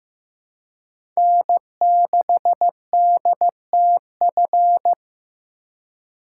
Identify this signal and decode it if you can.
Morse code, heard as a keyed tone, about 15 wpm, 700 Hz. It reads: N6DTF
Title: USA Call Signs - 5 Characters - 2x Character Spacing - 15wpm